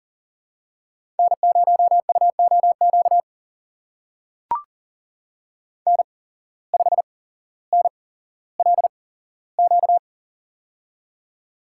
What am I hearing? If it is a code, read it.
Morse code, heard as a keyed tone, about 38 wpm, spaced as for 17 wpm, 700 Hz. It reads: N0UOQ N5NLQ
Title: USA Call Signs - 8x Character Spacing - 40wpm